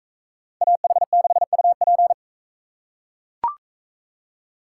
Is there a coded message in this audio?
Yes: Morse code from 0.6 to 2.1 s, heard as a keyed tone, about 40 wpm, 700 Hz. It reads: AH6UP